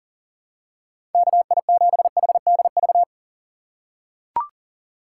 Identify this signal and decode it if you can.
Morse code, heard as a keyed tone, about 39 wpm, 700 Hz. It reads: KI7HDV
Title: USA Call Signs - 6 Characters - 2x Character Spacing - 40wpm